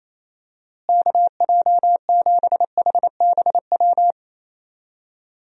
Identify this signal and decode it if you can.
Morse code, heard as a keyed tone, about 28 wpm, 700 Hz. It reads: KJ7HBW
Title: USA Call Signs - 6 Characters - Repeated 3x - 28wpm